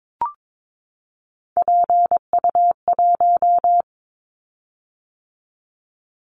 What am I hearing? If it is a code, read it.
Morse code, heard as a keyed tone, about 22 wpm, 700 Hz. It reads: PU1
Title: Callsign Prefixes for Top 100 DXCC - 22wpm